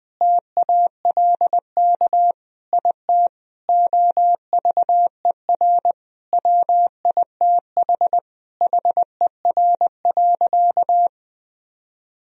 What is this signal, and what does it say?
Morse code, heard as a keyed tone, about 20 wpm, 700 Hz. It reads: TALK IT OVER WITH HER.
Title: Sentences from Top 1000 Words - Review Part 2 - 20wpm